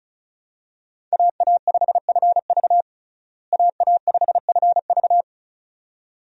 Morse code, heard as a keyed tone, about 36 wpm, 700 Hz. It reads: AA5FV AA5FV
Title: USA Call Signs - 5 Characters - Repeated 2x - 35wpm